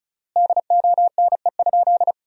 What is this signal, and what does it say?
Morse code, heard as a keyed tone, about 36 wpm, 700 Hz. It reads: DONE?
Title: 3x Word-Spacing - Sentences from Top 500 Words - 35wpm